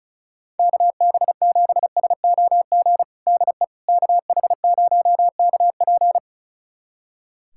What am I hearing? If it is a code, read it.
Morse code, heard as a keyed tone, about 34 wpm, 700 Hz. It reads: KB7SOG DE KH0KP
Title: USA Callsign DE Callsign 35wpm